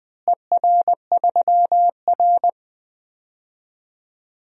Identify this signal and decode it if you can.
Morse code, heard as a keyed tone, about 20 wpm, 700 Hz. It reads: ER3R
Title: DXCC-100 Call Signs - 4 Characters - Repeated 3x - 20wpm